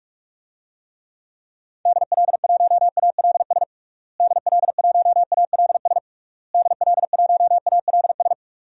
Morse code, heard as a keyed tone, about 46 wpm, 700 Hz. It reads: DL1ALS DL1ALS DL1ALS
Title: DXCC-100 Call Signs - 6 Characters - Repeated 3x - 45wpm